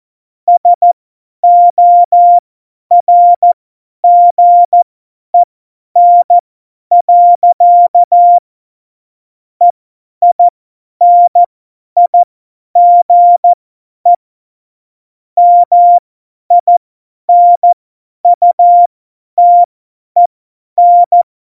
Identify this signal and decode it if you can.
Morse code, heard as a keyed tone, about 14 wpm, spaced as for 10 wpm, 700 Hz. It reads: SORGEN. EINIGE MINUTEN
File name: Morsekurs / Witze / Witz 22. Dezember 2020, 50 F70